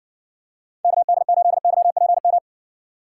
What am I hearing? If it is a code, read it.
Morse code, heard as a keyed tone, about 58 wpm, 700 Hz. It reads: KD8YPM